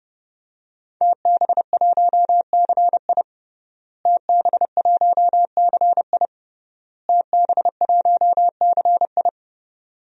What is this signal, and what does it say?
Morse code, heard as a keyed tone, about 32 wpm, 700 Hz. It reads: TB1CI TB1CI TB1CI
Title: DXCC-100 Call Signs - 5 Characters - Repeated 3x - 30wpm